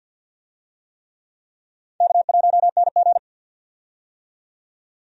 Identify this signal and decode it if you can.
Morse code, heard as a keyed tone, about 47 wpm, 700 Hz. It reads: K1NG
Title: USA Call Signs - 4 Characters - 50wpm